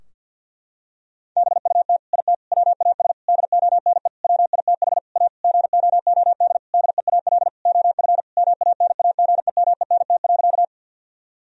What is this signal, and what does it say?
Morse code, heard as a keyed tone, about 50 wpm, 700 Hz. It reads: BUT IT WAS DONE WITH A GOOD DEAL OF MANAGEMENT.
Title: Sentences from Top 1000 Words - Review Part 4 - 50wpm